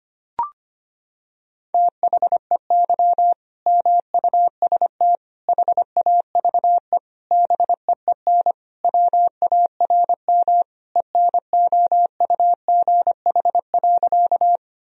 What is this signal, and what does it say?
Morse code, heard as a keyed tone, about 25 wpm, 700 Hz. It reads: THEY MUST HAVE BEEN WARM ENOUGH.
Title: Sentences from Top 1000 Words - Review Part 2 - 25wpm